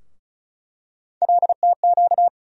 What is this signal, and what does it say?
Morse code, heard as a keyed tone, about 36 wpm, 700 Hz. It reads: LTQ
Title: USA Call Sign Postfixes - 3 Letters 35wpm